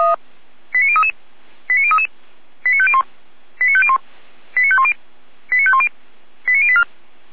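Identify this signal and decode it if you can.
Five-tone selective call (ZVEI), heard as quick runs of five tones, tones of about 70 ms. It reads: 89033 89033 89501 89501 89510 89510 89084
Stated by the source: ZVEI five-tone selective call received on SDR.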